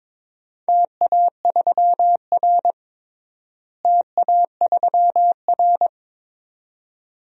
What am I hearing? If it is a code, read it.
Morse code, heard as a keyed tone, about 21 wpm, 700 Hz. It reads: TA3R TA3R
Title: DXCC-100 Call Signs - 4 Characters - Repeated 2x - 22wpm